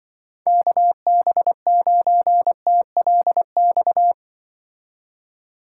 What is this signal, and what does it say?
Morse code, heard as a keyed tone, about 23 wpm, 700 Hz. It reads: KB9TLX
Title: Speed Racing - USA 2x3 Call Signs 20wpm